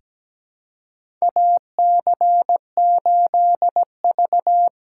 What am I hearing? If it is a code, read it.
Morse code, heard as a keyed tone, about 17 wpm, 700 Hz. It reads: AC8V
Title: USA Call Signs - 4 Characters - 4x Character Spacing - 17wpm